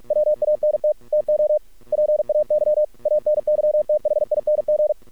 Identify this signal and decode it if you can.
Morse code, heard as a keyed tone, about 46 wpm, 600 Hz. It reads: WANT TO PAY ANYTHING
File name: archived / CRSnet